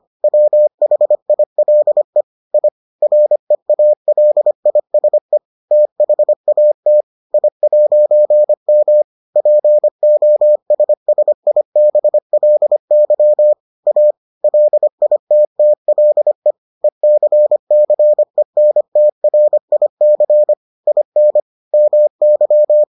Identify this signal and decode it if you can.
Morse code, heard as a keyed tone, about 25 wpm, 600 Hz. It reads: WHILE I REALISE THAT I'M POSSIBLY A LITTLE ECCENTRIC IN MY